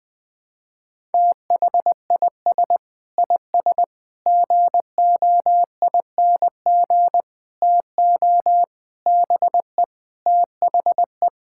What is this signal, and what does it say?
Morse code, heard as a keyed tone, about 20 wpm, 700 Hz. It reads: THIS IS GOING TO BE THE
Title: Sentences with 4-Word Phrases - Spoken Beginning - 20wpm